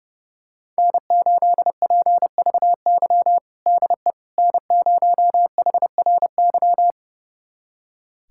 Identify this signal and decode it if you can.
Morse code, heard as a keyed tone, about 30 wpm, 700 Hz. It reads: N8PVY DE N0HRY
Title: USA Callsign DE Callsign - 5 Characters - 30wpm